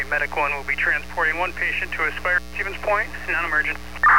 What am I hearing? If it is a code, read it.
Emergency Alert System (SAME) tones, heard as harsh buzzing bursts, with speech. It is heard from 0.6 to 3.7 s.